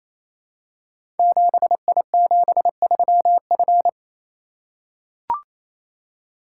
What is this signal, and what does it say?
Morse code, heard as a keyed tone, about 27 wpm, 700 Hz. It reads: 7I73F
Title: Code Groups - 5 Letter:Numbers - 28wpm